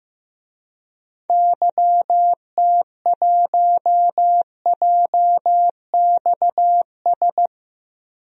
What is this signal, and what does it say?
Morse code, heard as a keyed tone, about 15 wpm, 700 Hz. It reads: YT1JXS
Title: DXCC-100 Call Signs - 6 Characters - Repeated 2x - 15wpm